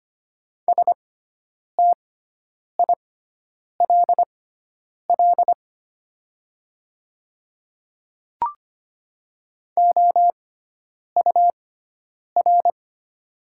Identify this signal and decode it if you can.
Morse code, heard as a keyed tone, about 25 wpm, spaced as for 9 wpm, 700 Hz. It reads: STILL OUR
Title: 6x Character Spacing - Top 300 Words - 25wpm